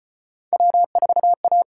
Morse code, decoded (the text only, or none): W4A